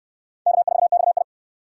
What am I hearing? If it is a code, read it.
Morse code, heard as a keyed tone, about 69 wpm, 700 Hz. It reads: K4XI